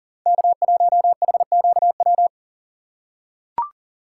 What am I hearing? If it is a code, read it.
Morse code, heard as a keyed tone, about 40 wpm, 700 Hz. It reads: K1HQW
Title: USA Call Signs - 5 Characters - 6x Character Spacing - 40wpm